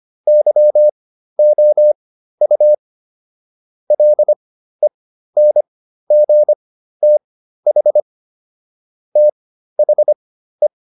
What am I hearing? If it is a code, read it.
Morse code, heard as a keyed tone, about 25 wpm, spaced as for 13 wpm, 600 Hz. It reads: YOU LENGTH THE